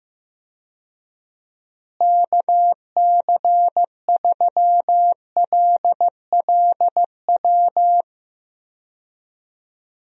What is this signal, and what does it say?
Morse code, heard as a keyed tone, about 14 wpm, 700 Hz. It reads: KC3LLW
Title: USA Call Signs - 6 Characters - Repeated 3x - 15wpm